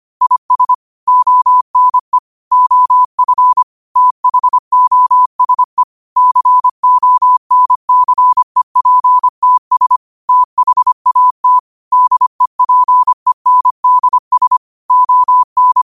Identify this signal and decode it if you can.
Morse code, heard as a keyed tone, about 25 wpm, 1000 Hz. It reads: IS ONE OF THOSE CONCEPTS THAT DEPENDS ON
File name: qotdfiles / 25wpm